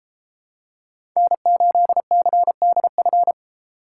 Morse code, heard as a keyed tone, about 32 wpm, 700 Hz. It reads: N8CDF